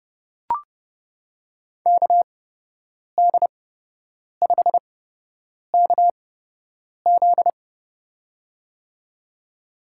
Morse code, heard as a keyed tone, about 30 wpm, spaced as for 8 wpm, 700 Hz. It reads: KD5KZ
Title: USA Call Signs - 5 Characters - 8x Character Spacing - 30wpm